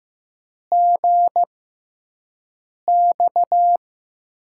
Morse code, heard as a keyed tone, about 15 wpm, 700 Hz. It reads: G X
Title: USA Call Signs - 6x Character Spacing - 15wpm